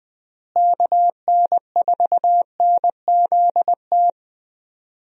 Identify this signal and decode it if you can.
Morse code, heard as a keyed tone, about 20 wpm, 700 Hz. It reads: KN4NZT
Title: USA Call Signs - 6 Characters - Repeated 3x - 20wpm